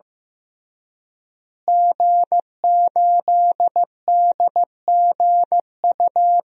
Morse code, heard as a keyed tone, about 15 wpm, 700 Hz. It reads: G8DGU
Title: DXCC-100 Call Signs - 5 Characters - Repeated 3x - 15wpm